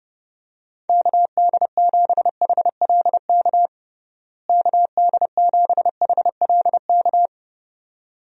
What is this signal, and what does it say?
Morse code, heard as a keyed tone, about 32 wpm, 700 Hz. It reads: KD7HLK KD7HLK
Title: USA Call Signs - 6 Characters - Repeated 2x - 30wpm